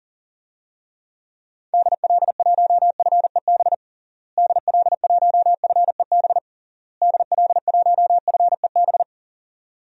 Morse code, heard as a keyed tone, about 40 wpm, 700 Hz. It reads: DL1FEB DL1FEB DL1FEB
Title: DXCC-100 Call Signs - 6 Characters - Repeated 3x - 40wpm